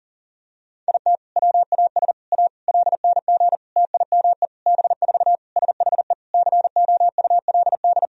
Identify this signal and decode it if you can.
Morse code, heard as a keyed tone, about 40 wpm, 700 Hz. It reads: IT WAS A LNG TIME B4 SHE COULD
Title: Sentences from Top 500 Words and Abbreviations 40wpm